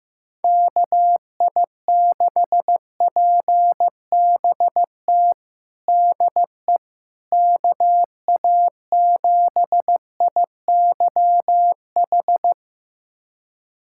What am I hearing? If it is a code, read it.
Morse code, heard as a keyed tone, about 15 wpm, 700 Hz. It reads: KI6PBT DE KA7IYH
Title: USA Callsign DE Callsign - 6 Characters - 15wpm